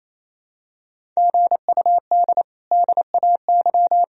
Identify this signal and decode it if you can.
Morse code, heard as a keyed tone, about 28 wpm, 700 Hz. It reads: GUD DAY
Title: Sentences from Top 200 Words and Abbreviations 28wpm